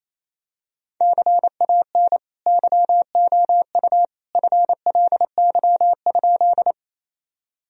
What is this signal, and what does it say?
Morse code, heard as a keyed tone, about 28 wpm, 700 Hz. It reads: CAN YOU FLY?
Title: Sentences from Top 1000 Words - Review Part 2 - 28wpm